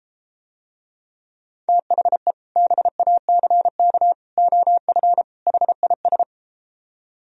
Morse code, heard as a keyed tone, about 33 wpm, 700 Hz. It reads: THE BACK OF HIS